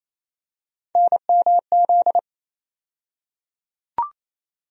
Morse code, heard as a keyed tone, about 28 wpm, 700 Hz. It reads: NMZ
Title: USA Call Sign Postfixes - 3 Letters 28wpm